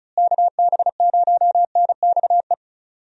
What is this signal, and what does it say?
Morse code, heard as a keyed tone, about 36 wpm, 700 Hz. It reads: KB0NXE
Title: USA Call Signs - 6 Characters - Repeated 3x - 35wpm